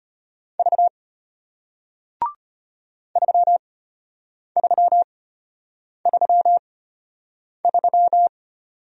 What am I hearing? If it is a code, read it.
Morse code, heard as a keyed tone, about 33 wpm, spaced as for 8 wpm, 700 Hz. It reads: V 3333